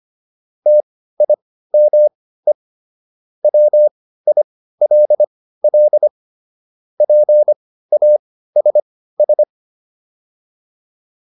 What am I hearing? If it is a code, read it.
Morse code, heard as a keyed tone, about 25 wpm, spaced as for 15 wpm, 600 Hz. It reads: TIME WILL PASS